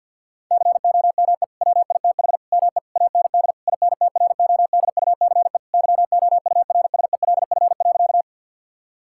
Morse code, heard as a keyed tone, about 50 wpm, 700 Hz. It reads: COME WITH ME AND INTRODUCE YOURSELF.
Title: Sentences from Top 1600 Words 50wpm